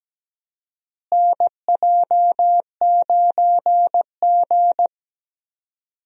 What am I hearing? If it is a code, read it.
Morse code, heard as a keyed tone, about 17 wpm, 700 Hz. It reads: NJ9G
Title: USA Call Signs - 4 Characters - Repeated 2x - 17wpm